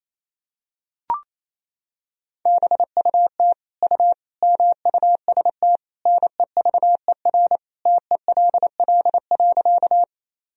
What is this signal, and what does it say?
Morse code, heard as a keyed tone, about 28 wpm, 700 Hz. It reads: BUT U MUST NEVER TELL.
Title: Sentences from Top 300 Words and Abbreviations 28wpm